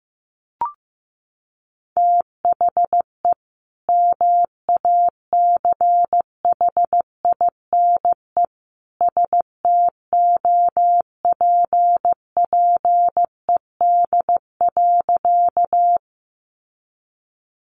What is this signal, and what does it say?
Morse code, heard as a keyed tone, about 15 wpm, 700 Hz. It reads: THE MACHINE STOPPED.
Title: A Sound Of Thunder 15wpm